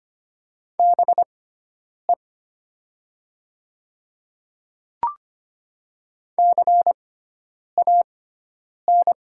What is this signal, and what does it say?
Morse code, heard as a keyed tone, about 25 wpm, spaced as for 9 wpm, 700 Hz. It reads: BE CAN
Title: 6x Character Spacing - Top 100 Words - 25wpm